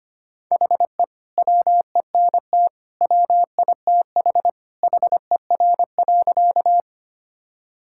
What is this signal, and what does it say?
Morse code, heard as a keyed tone, about 25 wpm, 700 Hz. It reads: HE WENT WITH HER.
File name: practice / 25wpm / Sentences with 2-Word Phrases - 25wpm